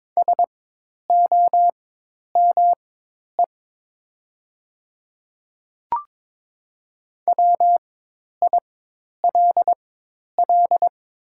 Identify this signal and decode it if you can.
Morse code, heard as a keyed tone, about 22 wpm, spaced as for 10 wpm, 700 Hz. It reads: SOME WILL
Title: 4x Character Spacing - Top 100 Words - 22wpm